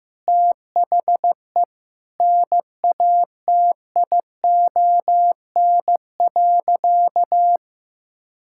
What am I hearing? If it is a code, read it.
Morse code, heard as a keyed tone, about 15 wpm, 700 Hz. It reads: THE NATION.
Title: Sentences from Top 1000 Words - Review Part 6 - 15wpm